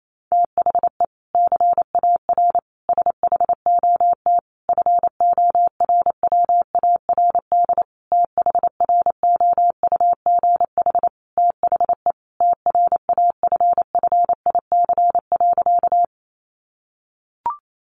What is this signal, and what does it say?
Morse code, heard as a keyed tone, about 28 wpm, 700 Hz. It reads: THE CAR SHOT FORWARD THROUGH THE TRAFFIC.
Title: Sentences from Top 1300 Words 28wpm